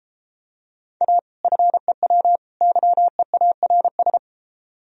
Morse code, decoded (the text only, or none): A FEW YEARS